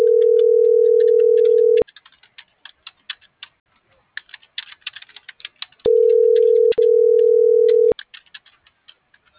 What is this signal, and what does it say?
Ringback tone (North America), 440 + 480 Hz.